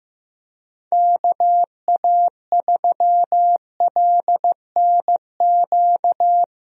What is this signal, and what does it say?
Morse code, heard as a keyed tone, about 15 wpm, 700 Hz. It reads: KA3LNQ
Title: USA Call Signs - 6 Characters - Repeated 2x - 15wpm